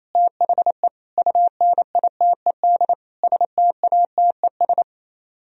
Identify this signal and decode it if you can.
Morse code, heard as a keyed tone, about 37 wpm, 700 Hz. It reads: THE UNITED STATES